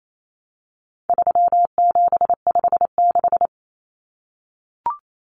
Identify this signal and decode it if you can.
Morse code, heard as a keyed tone, about 28 wpm, 700 Hz. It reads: 3756
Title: Common Cut Numbers - 4-Digit 28wpm